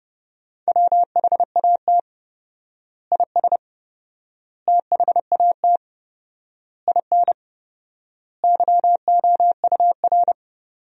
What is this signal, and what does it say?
Morse code, heard as a keyed tone, about 32 wpm, 700 Hz. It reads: WHAT IS THAT IN YOUR
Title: 4x Word-Spacing - Sentences from Top 200 Words - 30wpm